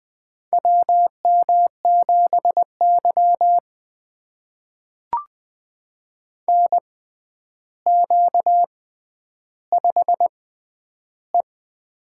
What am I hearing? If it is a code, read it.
Morse code, heard as a keyed tone, about 21 wpm, spaced as for 10 wpm, 700 Hz. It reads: WM7Y NQ5E
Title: USA Call Signs - 4 Characters - 6x Character Spacing - 20wpm